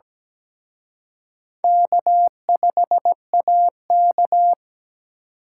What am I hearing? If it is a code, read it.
Morse code, heard as a keyed tone, about 20 wpm, 700 Hz. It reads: K5AK
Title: Speed Racing - USA 1x2 Call Signs 17wpm